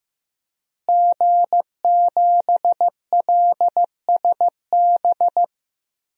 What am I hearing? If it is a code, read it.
Morse code, heard as a keyed tone, about 15 wpm, 700 Hz. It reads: G7LSB
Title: DXCC-100 Call Signs - 5 Characters - Repeated 2x - 15wpm